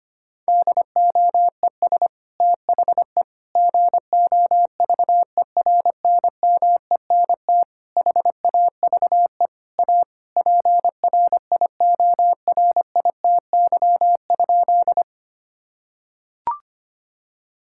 Morse code, heard as a keyed tone, about 25 wpm, 700 Hz. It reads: DOES THE GOVERNMENT HAVE A PRIORITY?
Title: Sentences from Top 1600 Words 25wpm